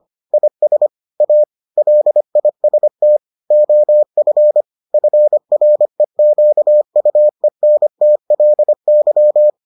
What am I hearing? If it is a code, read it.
Morse code, heard as a keyed tone, about 25 wpm, 600 Hz. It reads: IS A LIST OF FREQUENTLY